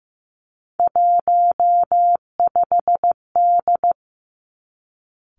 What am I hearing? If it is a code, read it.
Morse code, heard as a keyed tone, about 15 wpm, 700 Hz. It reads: 15D
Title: Three Challenging Letter-Number Combinations 15wpm